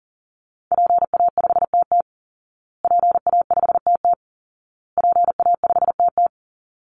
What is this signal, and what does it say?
Morse code, heard as a keyed tone, about 40 wpm, 700 Hz. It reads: PA5TT PA5TT PA5TT
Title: Call Signs Repeated 2x - 40wpm